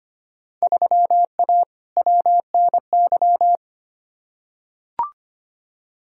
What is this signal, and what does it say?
Morse code, heard as a keyed tone, about 25 wpm, 700 Hz. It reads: 3A WNY
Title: Field Day Exchanges - 25wpm